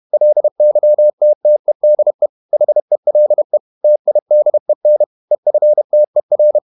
Morse code, heard as a keyed tone, about 31 wpm, 600 Hz. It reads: LYTTEDE HELE TIDEN EFTER
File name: Ferie QRQ 31wpm